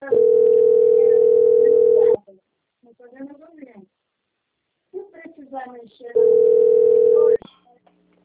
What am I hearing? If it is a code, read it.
Ringback tone (North America), 440 + 480 Hz, with speech.
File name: out- - - - - .15299